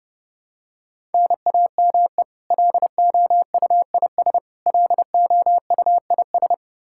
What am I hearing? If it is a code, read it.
Morse code, heard as a keyed tone, about 30 wpm, 700 Hz. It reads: NAME LOUIS LOUIS
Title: K1USN SST QSO Practice 30wpm